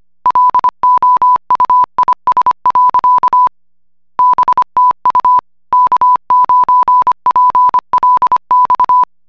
Morse code, heard as a keyed tone, about 25 wpm, 1000 Hz. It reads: LOUIS. BTU K9PLX